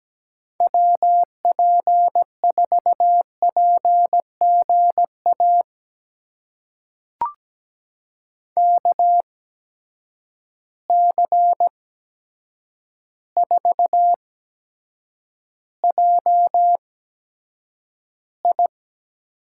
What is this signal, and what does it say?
Morse code, heard as a keyed tone, about 17 wpm, spaced as for 8 wpm, 700 Hz. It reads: WP4PGA KC4JI
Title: USA Call Signs - 6 Characters - 8x Character Spacing - 17wpm